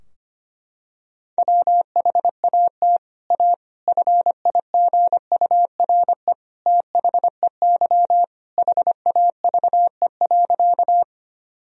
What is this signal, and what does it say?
Morse code, heard as a keyed tone, about 25 wpm, 700 Hz. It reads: WHAT A FIGURE THEY HAVE.
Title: Sentences from Top 1000 Words - Review Part 2 - 25wpm